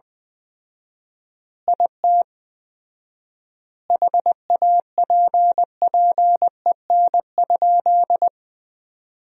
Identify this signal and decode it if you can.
Morse code, heard as a keyed tone, about 20 wpm, 700 Hz. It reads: IT HAPPEN?
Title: 4x Word-Spacing - Sentences from Top 400 Words - 20wpm